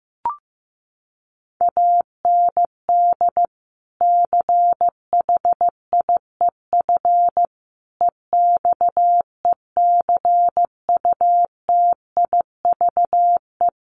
Morse code, heard as a keyed tone, about 15 wpm, 700 Hz. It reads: AND CHIEF EXECUTIVE
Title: Sets of 5 Words Encore2 15wpm